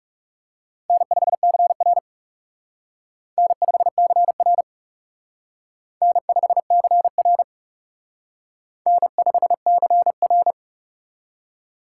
Morse code, heard as a keyed tone, about 37 wpm, 700 Hz. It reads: N5CR N5CR N5CR N5CR
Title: Speed Racing - USA Call Signs 30wpm